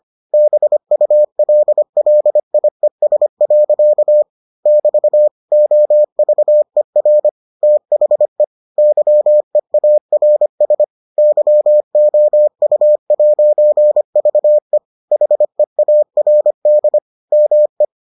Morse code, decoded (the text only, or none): BULLIES. <BT> OVER THE YEARS YOU'VE HEARD ME